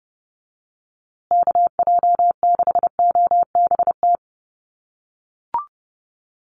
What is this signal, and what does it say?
Morse code, heard as a keyed tone, about 28 wpm, 700 Hz. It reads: KJ6OBT
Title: USA Call Signs Repeated 3x 30wpm